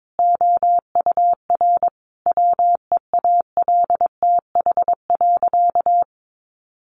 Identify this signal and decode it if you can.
Morse code, heard as a keyed tone, about 22 wpm, 700 Hz. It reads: OUR WEALTH.
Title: Sentences from Top 1200 Words 22wpm